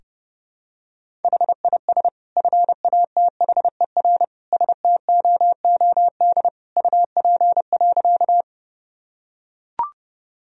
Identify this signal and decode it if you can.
Morse code, heard as a keyed tone, about 30 wpm, 700 Hz. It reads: HIS FATHER STOOD UP.
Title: Sentences from Top 1000 Words - Review Part 2 - 30wpm